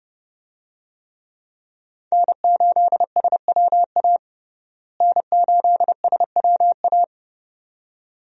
Morse code, heard as a keyed tone, about 29 wpm, 700 Hz. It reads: N8SWA N8SWA
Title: USA Call Signs - 5 Characters - Repeated 2x - 30wpm